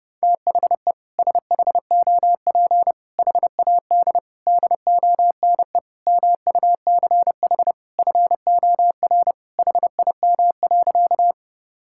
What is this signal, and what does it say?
Morse code, heard as a keyed tone, about 30 wpm, 700 Hz. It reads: THE SHOP HAD DONE MUCH FOR HIM.
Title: Sentences from Top 1600 Words 30wpm